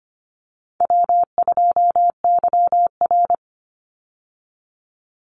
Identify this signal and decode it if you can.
Morse code, heard as a keyed tone, about 25 wpm, 700 Hz. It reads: W2YR
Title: Easy Call Signs Repeated 3x - 25wpm